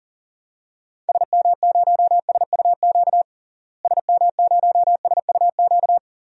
Morse code, heard as a keyed tone, about 40 wpm, 700 Hz. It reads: SM0SUQ SM0SUQ
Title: DXCC-100 Call Signs - 6 Characters - Repeated 3x - 40wpm